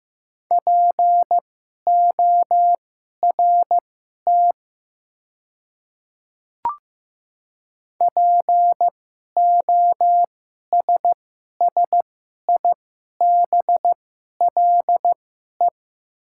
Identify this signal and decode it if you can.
Morse code, heard as a keyed tone, about 15 wpm, spaced as for 11 wpm, 700 Hz. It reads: PORT POSSIBLE
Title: 2x Character Spacing - Top 500 Words - 15wpm